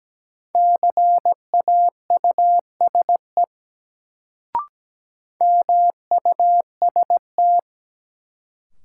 Morse code, heard as a keyed tone, about 17 wpm, 700 Hz. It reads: CAUSE MUST
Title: Sentences from Top 200 Words - New Word Drill - 17wpm